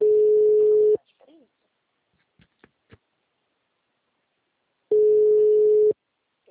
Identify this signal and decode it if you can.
Ringback tone (Europe, 425 Hz).